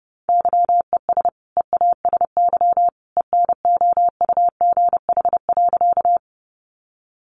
Morse code, heard as a keyed tone, about 30 wpm, 700 Hz. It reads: YES EASY ENOUGH.
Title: Sentences from Top 700 Words 30wpm